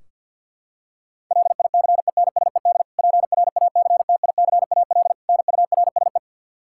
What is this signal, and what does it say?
Morse code, heard as a keyed tone, about 52 wpm, 700 Hz. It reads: LICENSED PRACTICAL NURSE
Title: Practice QSO - Exchange 3 - More Info - 50wpm